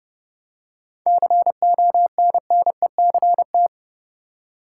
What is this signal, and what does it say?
Morse code, heard as a keyed tone, about 30 wpm, 700 Hz. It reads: CONNECT
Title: Seven Letter Words - 30wpm